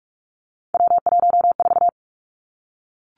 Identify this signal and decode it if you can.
Morse code, heard as a keyed tone, about 46 wpm, 700 Hz. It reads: W14
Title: Three Challenging Letter-Number Combinations 45wpm